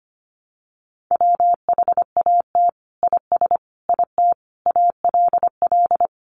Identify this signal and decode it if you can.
Morse code, heard as a keyed tone, about 25 wpm, 700 Hz. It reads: WHAT IS IT ALL
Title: Sentences from Top 100 Words 25wpm